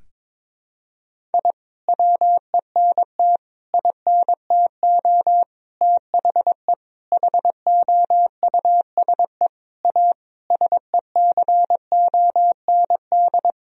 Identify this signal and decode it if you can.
Morse code, heard as a keyed tone, about 22 wpm, 700 Hz. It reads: I WENT INTO THE HOUSE A SECOND
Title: Sentences from Top 1000 Words - Review Part 1 - 22wpm